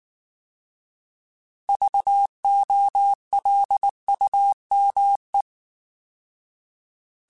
Morse code, heard as a keyed tone, about 19 wpm, 800 Hz. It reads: VOLUME